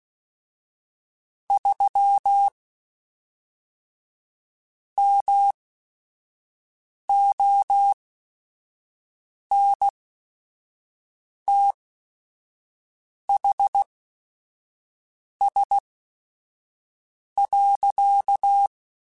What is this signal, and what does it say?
Morse code, heard as a keyed tone, about 16 wpm, spaced as for 5 wpm, 800 Hz. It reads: 3 MONTHS.
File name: QSOs / QSOs-mp3s / WD0BGP 1@5